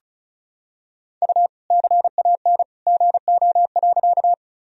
Morse code, heard as a keyed tone, about 36 wpm, 700 Hz. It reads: U CAN GO.
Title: Sentences from Top 100 Words and Abbreviations 35wpm